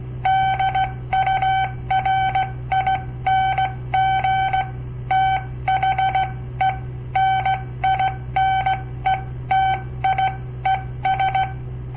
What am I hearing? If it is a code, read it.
Morse code, heard as a keyed tone, about 16 wpm, spaced as for 14 wpm, 770 Hz. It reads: DURING THE NINETIES
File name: papertapes / local / Code Practice-01